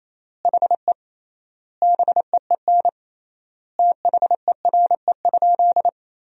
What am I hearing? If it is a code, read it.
Morse code, heard as a keyed tone, about 28 wpm, 700 Hz. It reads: HE BEEN THERE?
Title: 3x Word-Spacing - Sentences from Top 100 Words - 28wpm